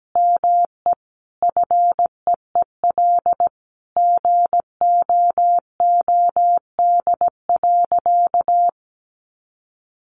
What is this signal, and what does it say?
Morse code, heard as a keyed tone, about 17 wpm, 700 Hz. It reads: ME FEEL GOOD.
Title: Sentences from Top 600 Words 17wpm